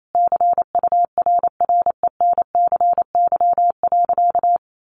Morse code, heard as a keyed tone, about 28 wpm, 700 Hz. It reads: CURRENCY.
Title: Sentences from Top 1500 Words 28wpm